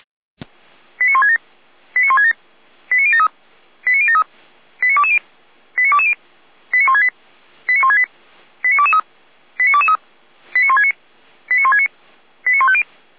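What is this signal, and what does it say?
Five-tone selective call (ZVEI), heard as quick runs of five tones, tones of about 80 ms. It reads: 89167 89167 89073 89073 89220 89220 89168 89168 89223 89223 89169 89169 89150
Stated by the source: ZVEI five-tone call received by SDR, from server of feuerwehr-orlamuende.org (fire brigade Orlamünde).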